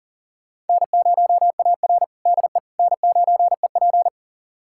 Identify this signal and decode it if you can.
Morse code, heard as a keyed tone, about 40 wpm, 700 Hz. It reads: N0AR DE N9EP
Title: USA Callsign DE Callsign - 4 Characters - 40wpm